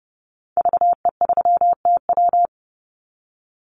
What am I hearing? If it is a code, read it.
Morse code, heard as a keyed tone, about 28 wpm, 700 Hz. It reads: VE3TW
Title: Call Signs - 30wpm